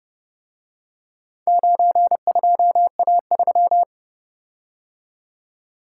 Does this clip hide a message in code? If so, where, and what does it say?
Yes: Morse code from 1.5 to 3.8 s, heard as a keyed tone, about 31 wpm, 700 Hz. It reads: 92A3